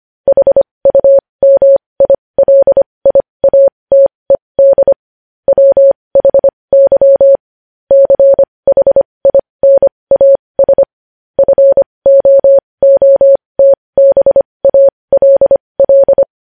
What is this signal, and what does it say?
Morse code, heard as a keyed tone, about 25 wpm, spaced as for 20 wpm, 550 Hz. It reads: HUMILIATED WHY CHINAS FOOTBALL
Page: news-headlines-in-morse-code-at-20-wpm